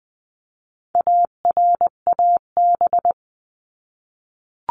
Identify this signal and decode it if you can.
Morse code, heard as a keyed tone, about 20 wpm, 700 Hz. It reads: ARAB
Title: 259 Lesson 42d - Words 18-20wpm